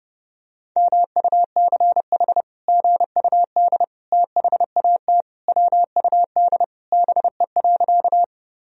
Morse code, heard as a keyed tone, about 32 wpm, 700 Hz. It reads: MUCH GUD THAT WUD BE.
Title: Sentences from Top 200 Words and Abbreviations 30wpm